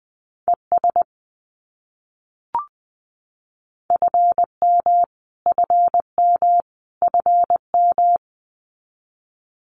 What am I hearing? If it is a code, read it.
Morse code, heard as a keyed tone, about 20 wpm, 700 Hz. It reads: ES FM FM FM
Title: Abbreviations Repeated 3x - 20wpm